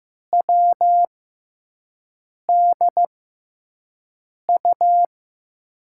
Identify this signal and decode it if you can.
Morse code, heard as a keyed tone, about 15 wpm, spaced as for 5 wpm, 700 Hz. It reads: WDU
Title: USA Call Signs - 6 Characters - 6x Character Spacing - 15wpm